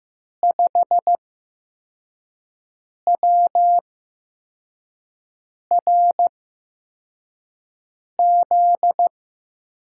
Morse code, heard as a keyed tone, about 15 wpm, spaced as for 4 wpm, 700 Hz. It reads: 5WRZ
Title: USA Call Signs - 5 Characters - 8x Character Spacing - 15wpm